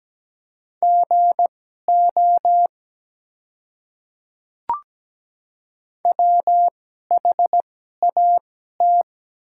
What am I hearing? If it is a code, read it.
Morse code, heard as a keyed tone, about 17 wpm, spaced as for 12 wpm, 700 Hz. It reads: GO WHAT
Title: 2x Character Spacing - Top 100 Words - 17wpm